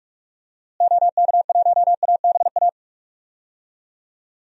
Morse code, heard as a keyed tone, about 46 wpm, 700 Hz. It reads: OK1ABA
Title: DXCC-100 Call Signs - 6 Characters - Repeated 3x - 45wpm